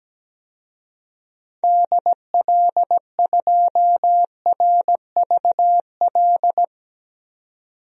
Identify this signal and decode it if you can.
Morse code, heard as a keyed tone, about 17 wpm, 700 Hz. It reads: DL2RVL
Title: DXCC-100 Call Signs - 6 Characters - 17wpm